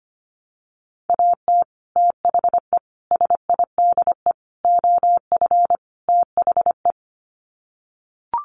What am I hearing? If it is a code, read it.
Morse code, heard as a keyed tone, about 25 wpm, 700 Hz. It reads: AT THE SIDE OF THE
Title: Sets of 5 Words Encore2 25wpm